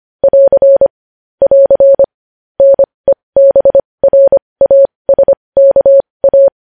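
Morse code, heard as a keyed tone, about 25 wpm, spaced as for 20 wpm, 550 Hz. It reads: <AR> <AR> NEBRASKA
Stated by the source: Morse code practice of news headlines at 20 wpm.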